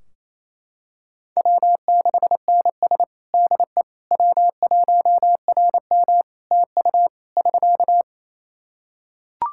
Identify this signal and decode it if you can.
Morse code, heard as a keyed tone, about 28 wpm, 700 Hz. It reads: W6NS DE W1RM TU <SK>
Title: Practice QSO - Exchange 4 - Ending - 28wpm